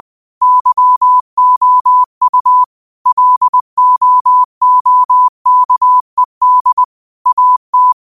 Morse code, heard as a keyed tone, about 20 wpm, 1000 Hz. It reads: YOU LOOKED AT
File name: qotdfiles / 20wpm